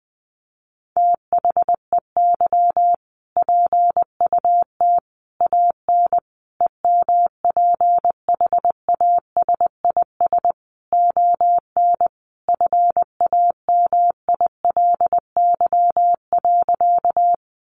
Morse code, heard as a keyed tone, about 20 wpm, 700 Hz. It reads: THEY PUT AN EMPHASIS ON FAMILY.
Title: Sentences from Top 1400 Words 20wpm